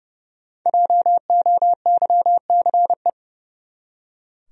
Morse code, heard as a keyed tone, about 32 wpm, 700 Hz. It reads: JOYCE
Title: USA Common Names - 30wpm